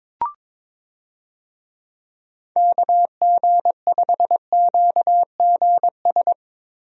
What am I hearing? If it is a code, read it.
Morse code, heard as a keyed tone, about 22 wpm, 700 Hz. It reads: KG5QGS
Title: USA Call Signs - 6 Characters - Repeated 3x - 22wpm